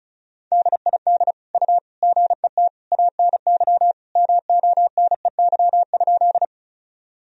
Morse code, heard as a keyed tone, about 34 wpm, 700 Hz. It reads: DID U GET ANY MONEY?
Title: Sentences from Top 400 Words and Abbreviations 35wpm